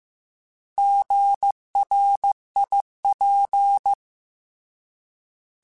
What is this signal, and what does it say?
Morse code, heard as a keyed tone, about 15 wpm, 800 Hz. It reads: GRIP